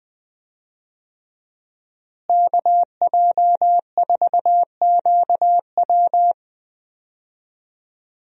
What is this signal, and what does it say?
Morse code, heard as a keyed tone, about 20 wpm, 700 Hz. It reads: KJ4QW
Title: USA Call Signs - 5 Characters - Repeated 3x - 20wpm